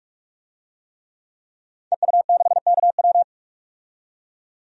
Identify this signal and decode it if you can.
Morse code, heard as a keyed tone, about 46 wpm, 700 Hz. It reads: EU6KW
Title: DXCC-100 Call Signs - 5 Characters - 45wpm